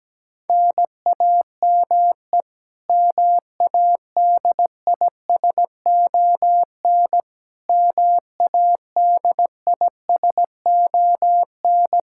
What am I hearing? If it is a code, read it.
Morse code, heard as a keyed tone, about 17 wpm, 700 Hz. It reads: NAME MADISON MADISON
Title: K1USN SST QSO Practice 17wpm